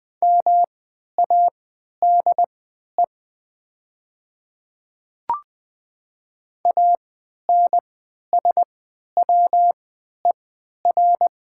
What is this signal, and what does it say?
Morse code, heard as a keyed tone, about 20 wpm, spaced as for 11 wpm, 700 Hz. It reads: MADE ANSWER
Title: 3x Character Spacing - Top 300 Words - 20wpm